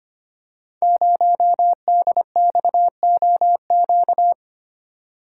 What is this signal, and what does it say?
Morse code, heard as a keyed tone, about 25 wpm, 700 Hz. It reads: 0DXOQ
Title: Code Groups - 5 Letter:Numbers - 25wpm